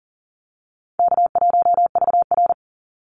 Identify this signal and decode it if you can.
Morse code, heard as a keyed tone, about 40 wpm, 700 Hz. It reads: K1VR
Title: Call Signs Repeated 2x - 40wpm